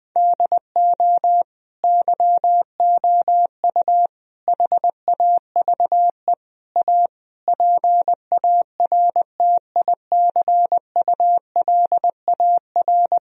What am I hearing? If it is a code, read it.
Morse code, heard as a keyed tone, about 21 wpm, 700 Hz. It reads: DO YOU HAVE A PARTICULAR
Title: Sentences from Top 1000 Words - Review Part 4 - 20wpm